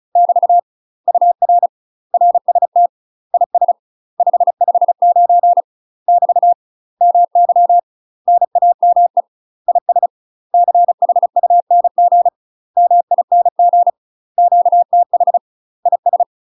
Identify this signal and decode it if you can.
Morse code, heard as a keyed tone, about 34 wpm, 700 Hz. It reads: <BT> UR RST IS 559 <BT> MY NAME IS CHUNG MING QTH IS
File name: UK MASTER QSOs TRAINING / 35WPM / UK MASTER QSOs 0004